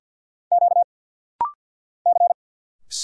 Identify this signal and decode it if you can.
Morse code, heard as a keyed tone, about 55 wpm, 700 Hz. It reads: Q C
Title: Single Letter-Number Alt - Rapid-Fire 50wpm